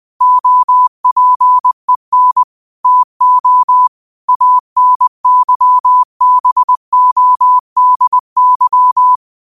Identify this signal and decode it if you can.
Morse code, heard as a keyed tone, about 20 wpm, 1000 Hz. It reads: OPEN TO ANYBODY